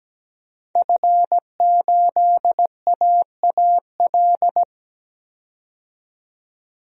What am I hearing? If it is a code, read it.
Morse code, heard as a keyed tone, about 17 wpm, 700 Hz. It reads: F8AAL
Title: DXCC-100 Call Signs - 5 Characters - 17wpm